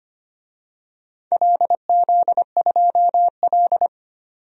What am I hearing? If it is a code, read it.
Morse code, heard as a keyed tone, about 25 wpm, 700 Hz. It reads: LZ2L
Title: DXCC-100 Call Signs - 4 Characters - 25wpm